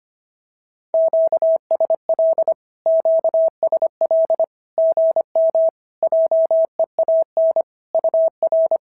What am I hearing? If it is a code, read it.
Morse code, heard as a keyed tone, about 25 wpm, 650 Hz. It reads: QSL QSL GM JEAN UR
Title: SOTA QSO Practice 25wpm